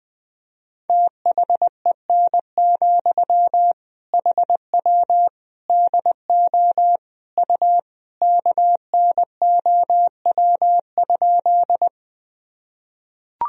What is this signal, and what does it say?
Morse code, heard as a keyed tone, about 21 wpm, 700 Hz. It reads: THEN, HW DO U KNOW?
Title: Sentences from Top 100 Words and Abbreviations 20wpm